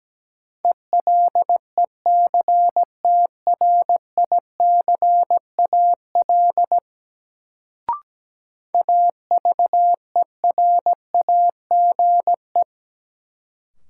Morse code, heard as a keyed tone, about 17 wpm, 700 Hz. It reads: ELECTRICAL AVERAGE
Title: Sentences from Top 1000 Words - Review Part 6 - New Word Drill - 17wpm